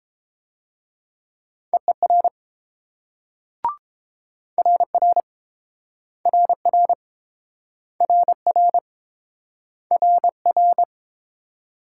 Morse code, heard as a keyed tone, about 30 wpm, 700 Hz. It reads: EER RR RR RR RR